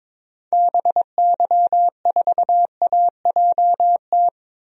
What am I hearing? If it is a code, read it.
Morse code, heard as a keyed tone, about 22 wpm, 700 Hz. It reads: BY4AJT
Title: DXCC-100 Call Signs - 6 Characters - Repeated 2x - 22wpm